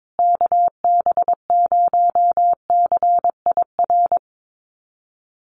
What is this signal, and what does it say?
Morse code, heard as a keyed tone, about 22 wpm, 700 Hz. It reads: KB0CIR
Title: USA Call Signs Repeated 3x 22wpm